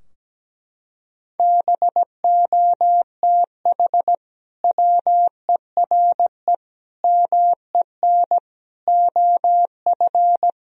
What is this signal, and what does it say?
Morse code, heard as a keyed tone, about 17 wpm, 700 Hz. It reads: BOTH WERE MEN OF
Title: Sentences from Top 1000 Words - Review Part 3 - 17wpm